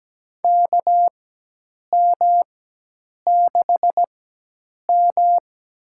Morse code, heard as a keyed tone, about 17 wpm, spaced as for 8 wpm, 700 Hz. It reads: KM6M
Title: USA Call Signs - 4x Character Spacing - 17wpm